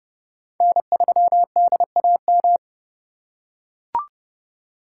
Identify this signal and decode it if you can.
Morse code, heard as a keyed tone, about 28 wpm, 700 Hz. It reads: N3DAM